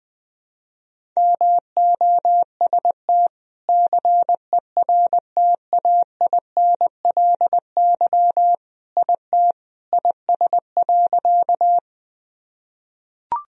Morse code, heard as a keyed tone, about 20 wpm, 700 Hz. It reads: MOST CERTAINLY IT IS.
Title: Sentences from Top 1000 Words - Review Part 1 - 20wpm